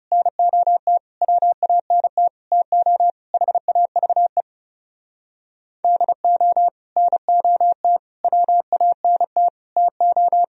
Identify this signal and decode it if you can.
Morse code, heard as a keyed tone, about 37 wpm, 700 Hz. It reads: NOT WANT TO HAVE DO NOT WANT TO
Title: Speed-Racing - Sets of 5 Words 30wpm